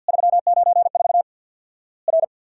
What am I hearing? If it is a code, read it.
Morse code, heard as a keyed tone, about 50 wpm, 700 Hz. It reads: 394 R
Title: Worked All Europe DX Contest - QTC Exchange with Repeats - 50wpm